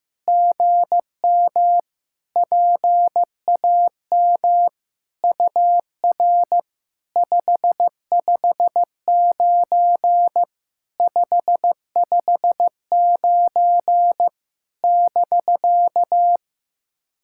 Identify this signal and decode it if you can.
Morse code, heard as a keyed tone, about 15 wpm, 700 Hz. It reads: GM PAM UR 559 559 <BK>
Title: SOTA QSO Practice 15wpm